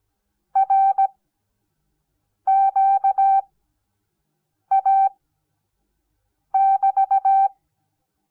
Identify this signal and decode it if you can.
Morse code, heard as a keyed tone, about 17 wpm, spaced as for 5 wpm, 780 Hz. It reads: RQA<BT>
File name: Onlinelehrgang / morsen / DJ4UF Morsen06